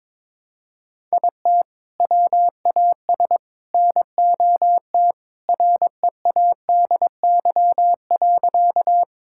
Morse code, heard as a keyed tone, about 22 wpm, 700 Hz. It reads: IT WAS NOT READY.
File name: practice / 22wpm / Sentences with 2-Word Phrases - 22wpm